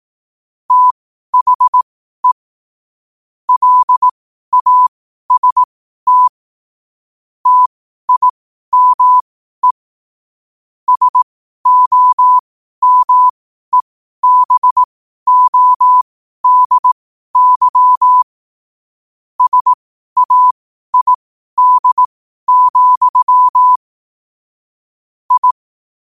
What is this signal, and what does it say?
Morse code, heard as a keyed tone, about 18 wpm, spaced as for 12 wpm, 1000 Hz. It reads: THE LAST TIME SOMEBODY SAID, I